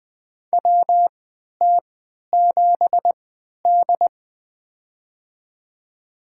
Morse code, heard as a keyed tone, about 20 wpm, spaced as for 11 wpm, 700 Hz. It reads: WT7D
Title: USA Call Signs - 4 Characters - 3x Character Spacing - 20wpm